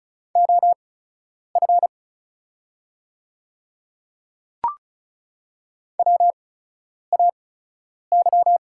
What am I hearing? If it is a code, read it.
Morse code, heard as a keyed tone, about 36 wpm, spaced as for 10 wpm, 700 Hz. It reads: OF WAY
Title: 8x Character Spacing - Top 100 Words - 35wpm